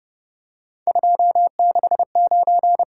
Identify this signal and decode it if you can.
Morse code, heard as a keyed tone, about 29 wpm, 700 Hz. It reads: 269
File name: Speed-Racing - Three Letter-Number Combinations - 25wpm